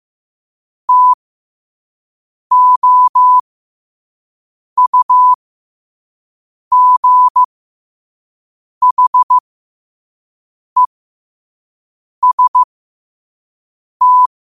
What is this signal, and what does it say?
Morse code, heard as a keyed tone, about 15 wpm, spaced as for 5 wpm, 1000 Hz. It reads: TOUGHEST
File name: qotdfiles / 5wpm